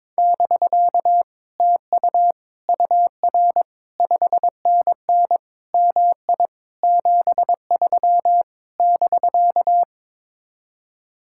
Morse code, heard as a keyed tone, about 21 wpm, 700 Hz. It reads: <BK> TU UR 5NN MI 73 <BK>
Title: Speed-Racing Common QSO Phrases - 22wpm